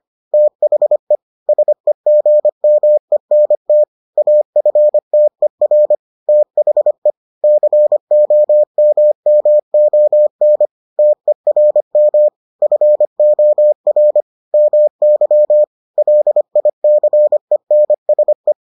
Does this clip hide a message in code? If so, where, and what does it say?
Yes: Morse code from 0.3 to 18.5 s, heard as a keyed tone, about 25 wpm, 600 Hz. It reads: THE SEGMENT AFTER THE COMMON TERM FOR MY LICENSE